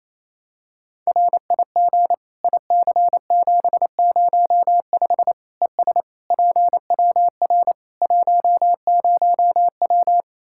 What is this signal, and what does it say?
Morse code, heard as a keyed tone, about 27 wpm, 700 Hz. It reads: RIG IC705 ES PWR 10W
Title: Practice QSO - Exchange 2 - Info - 28wpm